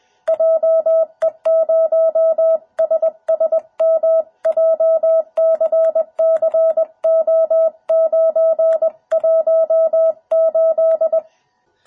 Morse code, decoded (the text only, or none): JE0SSM JCCO918